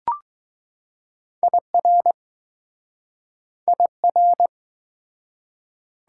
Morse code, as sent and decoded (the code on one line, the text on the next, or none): .. .-. / .. .-.
IR IR